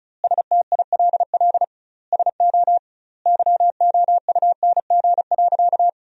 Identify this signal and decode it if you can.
Morse code, heard as a keyed tone, about 36 wpm, 700 Hz. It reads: STILL SO YOUNG.
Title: 2x Word-Spacing - Sentences from Top 400 Words - 35wpm